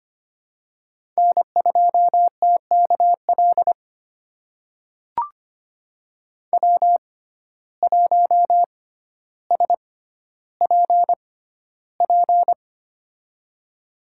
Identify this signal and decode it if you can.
Morse code, heard as a keyed tone, about 25 wpm, spaced as for 13 wpm, 700 Hz. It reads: N2TKL W1SPP
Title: USA Call Signs - 5 Characters - 6x Character Spacing - 25wpm